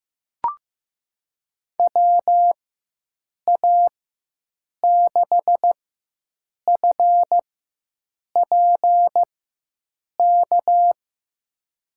Morse code, heard as a keyed tone, about 15 wpm, spaced as for 7 wpm, 700 Hz. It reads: WA6FPK
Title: USA Call Signs - 6 Characters - 4x Character Spacing - 15wpm